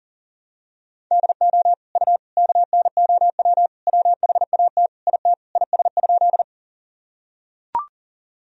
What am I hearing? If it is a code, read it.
Morse code, heard as a keyed tone, about 40 wpm, 700 Hz. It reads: DO U KNOW WHAT IT IS?
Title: Sentences from Top 100 Words and Abbreviations 40wpm